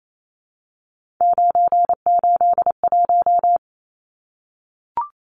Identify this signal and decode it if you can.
Morse code, heard as a keyed tone, about 28 wpm, 700 Hz. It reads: 981
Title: Numbers - 3-Digit 28wpm